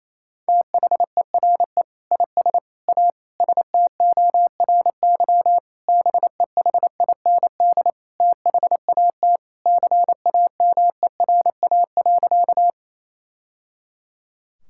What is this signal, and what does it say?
Morse code, heard as a keyed tone, about 27 wpm, 700 Hz. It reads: THERE IS A STORY BEHIND THAT CAMERA.
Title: Sentences from Top 1000 Words - Review Part 5 - 28wpm